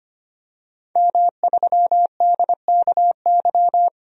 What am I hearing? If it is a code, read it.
Morse code, heard as a keyed tone, about 25 wpm, 700 Hz. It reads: M3DKY
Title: DXCC-100 Call Signs - 5 Characters - 25wpm